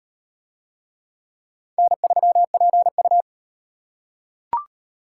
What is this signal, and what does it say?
Morse code, heard as a keyed tone, about 38 wpm, 700 Hz. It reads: N3PU